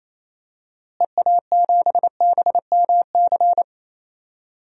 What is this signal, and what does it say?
Morse code, heard as a keyed tone, about 28 wpm, 700 Hz. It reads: EA7BMC
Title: DXCC-100 Call Signs - 6 Characters - Repeated 2x - 28wpm